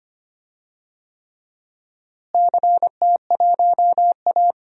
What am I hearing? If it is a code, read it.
Morse code, heard as a keyed tone, about 25 wpm, 700 Hz. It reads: CT1A
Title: DXCC-100 Call Signs - 4 Characters - Repeated 3x - 25wpm